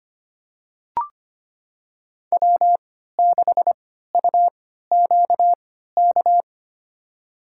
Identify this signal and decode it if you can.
Morse code, heard as a keyed tone, about 25 wpm, spaced as for 14 wpm, 700 Hz. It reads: W6UQK
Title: USA Call Signs - 5 Characters - 3x Character Spacing - 25wpm